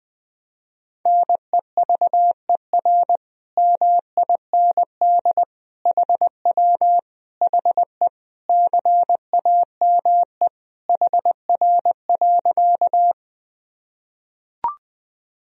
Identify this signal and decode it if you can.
Morse code, heard as a keyed tone, about 20 wpm, 700 Hz. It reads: NEVER MIND HW HE CAME HR.
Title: Sentences from Top 500 Words and Abbreviations 20wpm